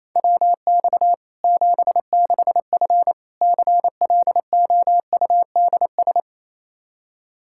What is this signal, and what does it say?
Morse code, heard as a keyed tone, about 28 wpm, 700 Hz. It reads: WX 76F CLOUDS
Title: Practice QSO - Exchange 2 - Info - 28wpm